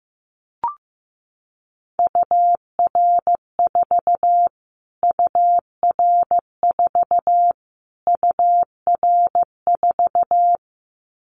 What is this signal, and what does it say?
Morse code, heard as a keyed tone, about 15 wpm, 700 Hz. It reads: UR4 UR4 UR4
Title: Callsign Prefixes for Top 25 DXCC - Repeated 3x - 15wpm